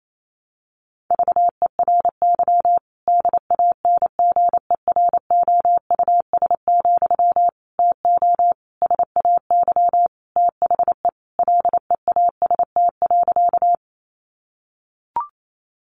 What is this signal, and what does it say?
Morse code, heard as a keyed tone, about 28 wpm, 700 Hz. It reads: VERY DANGEROUS, TO SAY THE LEAST.
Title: Sentences from Top 1200 Words 28wpm